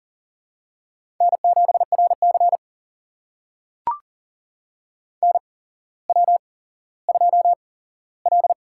Morse code, heard as a keyed tone, about 40 wpm, spaced as for 17 wpm, 700 Hz. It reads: N7RC NW2L
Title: USA Call Signs - 4 Characters - 8x Character Spacing - 40wpm